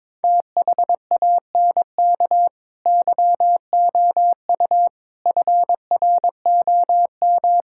Morse code, heard as a keyed tone, about 22 wpm, 700 Hz. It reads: THANK YOU FROM
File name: practice / 22wpm / Sets of 8 Words - 22wpm